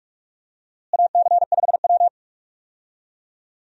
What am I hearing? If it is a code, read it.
Morse code, heard as a keyed tone, about 46 wpm, 700 Hz. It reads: AC5W
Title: USA Call Signs - 4 Characters - Repeated 2x - 45wpm